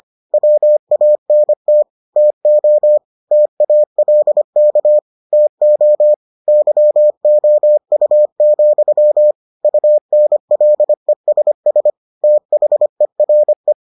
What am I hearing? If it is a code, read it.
Morse code, heard as a keyed tone, about 25 wpm, 600 Hz. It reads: WANT TO TALK TO YOU, UNLESS THERE